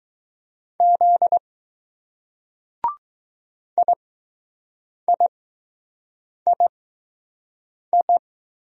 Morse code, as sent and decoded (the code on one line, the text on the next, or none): --.. / .. .. .. ..
Z IIII